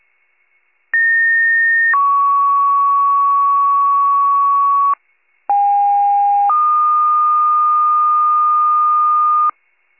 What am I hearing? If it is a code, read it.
Two-tone page, heard as a one-second tone, then a three-second tone, about 1850 then 1120 Hz, about 810 then 1220 Hz. It is heard from 0.9 to 9.5 s.
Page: County Wide Paging Archive